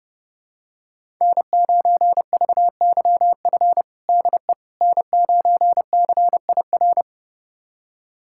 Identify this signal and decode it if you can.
Morse code, heard as a keyed tone, about 30 wpm, 700 Hz. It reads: N9VYF DE N9CIR
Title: USA Callsign DE Callsign - 5 Characters - 30wpm